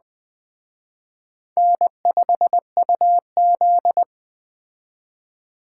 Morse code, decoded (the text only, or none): N5UZ